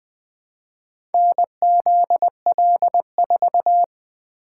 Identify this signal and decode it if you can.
Morse code, heard as a keyed tone, about 21 wpm, 700 Hz. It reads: NZL4
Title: Four Letter-Number Combinations - 20wpm